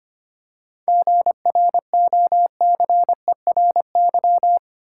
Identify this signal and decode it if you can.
Morse code, heard as a keyed tone, about 25 wpm, 700 Hz. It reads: GROCERY
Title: Seven Letter Words - 25wpm